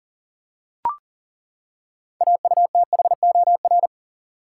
Morse code, heard as a keyed tone, about 38 wpm, 700 Hz. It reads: AUTHOR
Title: Six Letter Words - 40wpm